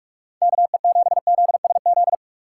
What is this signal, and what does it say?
Morse code, heard as a keyed tone, about 45 wpm, 700 Hz. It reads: KE7ZSZ